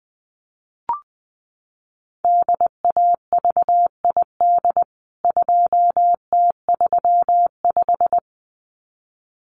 Morse code, heard as a keyed tone, about 20 wpm, 700 Hz. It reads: DAVID 2T35
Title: CWT Exchange 20wpm